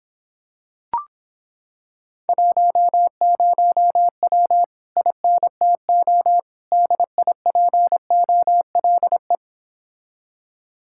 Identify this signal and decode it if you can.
Morse code, heard as a keyed tone, about 26 wpm, 700 Hz. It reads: 10W INTO DIPOLE